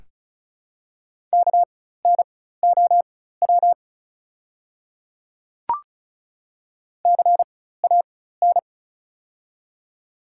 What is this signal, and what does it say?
Morse code, heard as a keyed tone, about 36 wpm, spaced as for 16 wpm, 700 Hz. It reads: KNOW CAN
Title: 4x Character Spacing - Top 100 Words - 35wpm